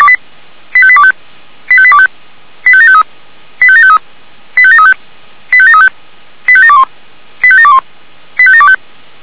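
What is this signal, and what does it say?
Five-tone selective call (ZVEI), heard as quick runs of five tones, tones of about 70 ms. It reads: 85825 85825 85753 85753 85735 85735 85721 85721 85725
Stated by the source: ZVEI five-tone selective call received by SDR, from the Feuerwehr Orlamünde (fire brigade) audio server.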